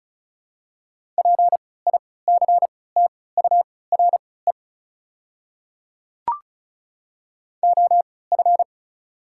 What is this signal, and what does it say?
Morse code, heard as a keyed tone, about 36 wpm, spaced as for 20 wpm, 700 Hz. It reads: PICTURE OF
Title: 3x Character Spacing - Top 200 Words - 35wpm